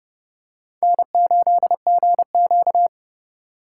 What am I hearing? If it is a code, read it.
Morse code, heard as a keyed tone, about 29 wpm, 700 Hz. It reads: N8GQ